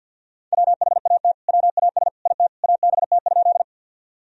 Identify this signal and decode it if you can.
Morse code, heard as a keyed tone, about 50 wpm, 700 Hz. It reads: WHAT WAS IT ABT?
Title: Sentences from Top 100 Words and Abbreviations 50wpm